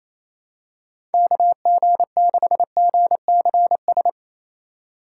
Morse code, heard as a keyed tone, about 28 wpm, 700 Hz. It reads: KG6GCS
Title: USA Call Signs - 6 Characters - 3x Character Spacing - 28wpm